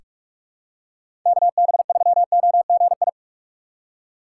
Morse code, heard as a keyed tone, about 43 wpm, 700 Hz. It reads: KB3OGI
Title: USA Call Signs - 6 Characters - 2x Character Spacing - 45wpm